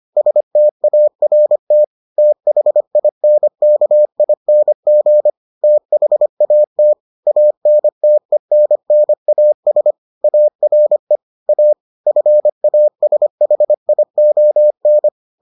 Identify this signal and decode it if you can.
Morse code, heard as a keyed tone, about 25 wpm, 600 Hz. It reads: START THINKING THAT ANTENNAS ARE A FASHION